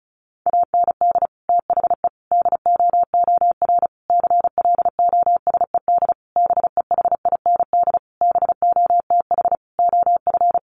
Morse code, heard as a keyed tone, about 34 wpm, 700 Hz. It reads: AND THE DOOR CLOSED BEHIND BOTH OF
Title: Sentences from Top 1400 Words 35wpm